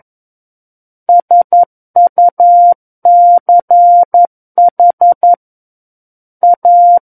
Morse code, heard as a keyed tone, about 11 wpm, 700 Hz. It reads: SUCH A
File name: 11 WPM